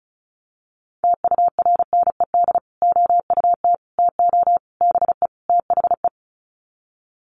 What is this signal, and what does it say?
Morse code, heard as a keyed tone, about 34 wpm, 700 Hz. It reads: TURNED OUT TO BE THE
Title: Sets of 5 Words Encore 35wpm